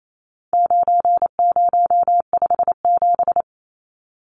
Morse code, heard as a keyed tone, about 28 wpm, 700 Hz. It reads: 9057
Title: Numbers - 4-Digit 28wpm